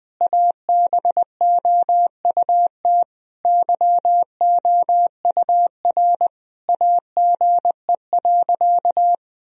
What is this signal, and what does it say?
Morse code, heard as a keyed tone, about 20 wpm, 700 Hz. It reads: ABOUT YOUR AGE.
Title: Sentences from Top 1000 Words - Review Part 3 - 20wpm